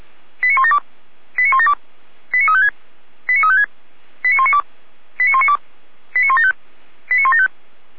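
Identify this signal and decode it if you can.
Five-tone selective call (ZVEI), heard as quick runs of five tones, tones of about 80 ms. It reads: 89172 89172 89356 89356 89192 89192 89175 89175
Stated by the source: ZVEI five-tone selective call received by SDR, from the audio server of Feuerwehr Orlamünde (fire brigade).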